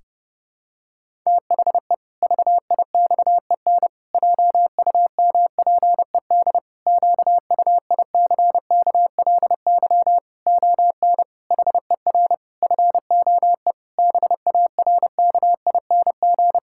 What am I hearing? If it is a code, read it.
Morse code, heard as a keyed tone, about 30 wpm, 700 Hz. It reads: THE VIXEN JUMPED QUICKLY ON HER FOE BARKING
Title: Pangram Challenge v2 - 30wpm